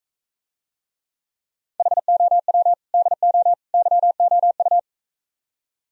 Morse code, heard as a keyed tone, about 45 wpm, 700 Hz. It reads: HOW DO YOU